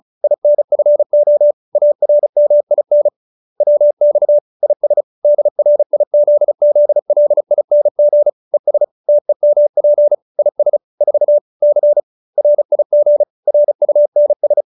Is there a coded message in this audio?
Yes: Morse code from 0.2 to 14.6 s, heard as a keyed tone, about 34 wpm, 590 Hz. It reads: INFO ARMIN WX IS DRIZZLING ES TEMP IS 4 C RIG RUNS